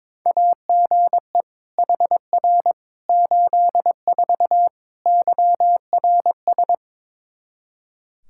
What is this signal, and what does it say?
Morse code, heard as a keyed tone, about 22 wpm, 700 Hz. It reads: AGE HR 84 YRS
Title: Practice QSO - Exchange 3 - More Info - 22wpm